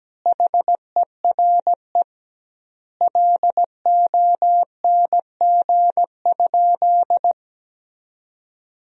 Morse code, heard as a keyed tone, about 17 wpm, 700 Hz. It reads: HERE LONG?
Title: 2x Word-Spacing - Sentences from Top 200 Words - 17wpm